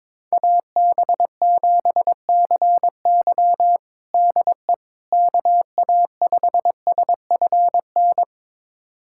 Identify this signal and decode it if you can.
Morse code, heard as a keyed tone, about 21 wpm, 700 Hz. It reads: AB7CY DE KA5SFN
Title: USA Callsign DE Callsign 22wpm